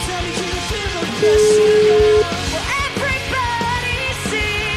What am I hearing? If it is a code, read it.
Ringback tone (Europe, 425 Hz), with noise.